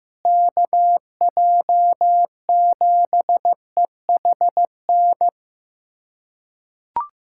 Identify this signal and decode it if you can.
Morse code, heard as a keyed tone, about 15 wpm, 700 Hz. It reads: KJ7EHN
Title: USA Callsign DE Callsign 15wpm